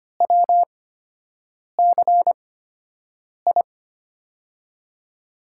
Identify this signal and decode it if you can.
Morse code, heard as a keyed tone, about 25 wpm, spaced as for 7 wpm, 700 Hz. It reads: WCI
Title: USA Call Signs - 5 Characters - 8x Character Spacing - 25wpm